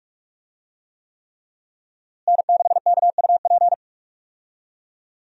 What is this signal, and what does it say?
Morse code, heard as a keyed tone, about 44 wpm, 700 Hz. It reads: N6KUP
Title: USA Call Signs - 5 Characters - Repeated 3x - 45wpm